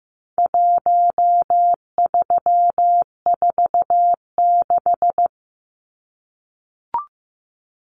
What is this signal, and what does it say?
Morse code, heard as a keyed tone, about 15 wpm, 700 Hz. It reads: 1346
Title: Numbers - Contest Style - 4-Digit 15wpm